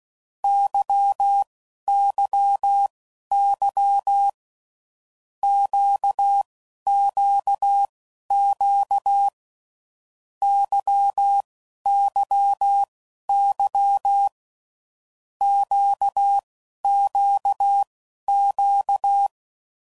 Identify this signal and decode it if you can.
Morse code, heard as a keyed tone, about 16 wpm, spaced as for 11 wpm, 800 Hz. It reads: YYY QQQ YYY QQQ
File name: Groups / Basic Letter Groups / LFCYQ